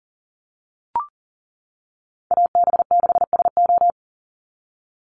Morse code, heard as a keyed tone, about 40 wpm, 700 Hz. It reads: AB6SO
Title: NA SOTA Chaser Callsigns 40wpm